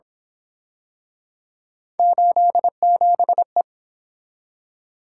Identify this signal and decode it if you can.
Morse code, heard as a keyed tone, about 27 wpm, 700 Hz. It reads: 87E
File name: Speed-Racing - Three Letter-Number Combinations - 22wpm